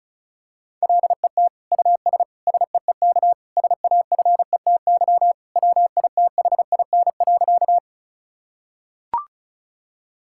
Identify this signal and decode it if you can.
Morse code, heard as a keyed tone, about 36 wpm, 700 Hz. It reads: LET US SEEK SAFETY WITHIN.
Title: Sentences from Top 1600 Words 35wpm